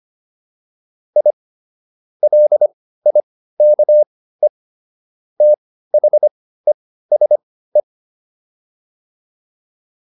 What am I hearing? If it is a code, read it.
Morse code, heard as a keyed tone, about 25 wpm, spaced as for 15 wpm, 600 Hz. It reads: I LIKE THESE